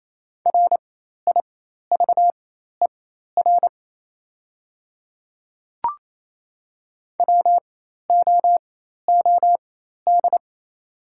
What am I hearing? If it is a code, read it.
Morse code, heard as a keyed tone, about 28 wpm, spaced as for 13 wpm, 700 Hz. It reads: RIVER WOOD
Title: 4x Character Spacing - Top 300 Words - 28wpm